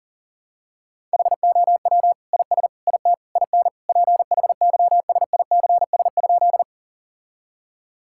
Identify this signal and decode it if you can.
Morse code, heard as a keyed tone, about 40 wpm, 700 Hz. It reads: HOW IS IT IN PHYSICS?
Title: Sentences from Top 1000 Words - Review Part 5 - 40wpm